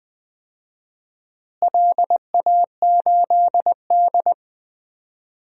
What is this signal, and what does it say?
Morse code, heard as a keyed tone, about 20 wpm, 700 Hz. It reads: LA8D
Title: DXCC-100 Call Signs - 4 Characters - Repeated 2x - 20wpm